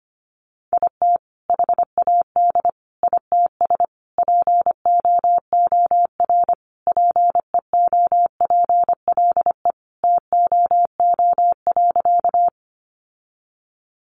Morse code, heard as a keyed tone, about 25 wpm, 700 Hz. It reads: IT HAD ITS POOR PEOPLE TOO.
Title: Sentences from Top 1000 Words 25wpm